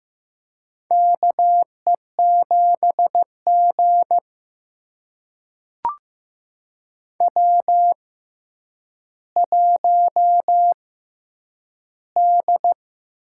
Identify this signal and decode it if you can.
Morse code, heard as a keyed tone, about 15 wpm, spaced as for 9 wpm, 700 Hz. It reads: KE7G W1D
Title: USA Call Signs - 4 Characters - 6x Character Spacing - 15wpm